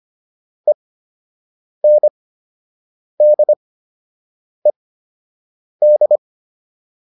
Morse code, heard as a keyed tone, about 25 wpm, spaced as for 7 wpm, 600 Hz. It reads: ENDED